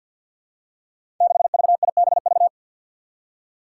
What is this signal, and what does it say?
Morse code, heard as a keyed tone, about 53 wpm, 700 Hz. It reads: 6VIBV
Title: Five Challenging Letter-Number Combinations 50wpm